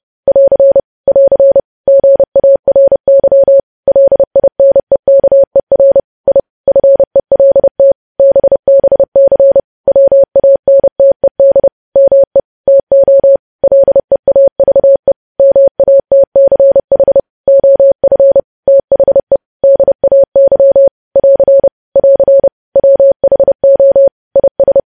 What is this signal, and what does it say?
Morse code, heard as a keyed tone, about 30 wpm, 550 Hz. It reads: <AR> <AR> GARY LINEKER I FELT BBC WANTED ME TO LEAVE MATCH OF THE DAY <AR> <AR> WHO IS
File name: podcast / News headlines-1x-30x30